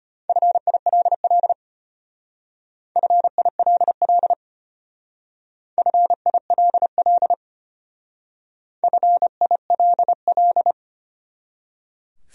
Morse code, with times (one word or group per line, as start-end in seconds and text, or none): 0.3-1.5 FILL
3.0-4.3 FILL
5.8-7.3 FILL
8.8-10.7 FILL